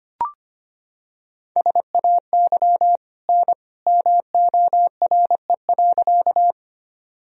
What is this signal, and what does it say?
Morse code, heard as a keyed tone, about 25 wpm, 700 Hz. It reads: SAY N MORE.
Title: Sentences from Top 200 Words and Abbreviations 25wpm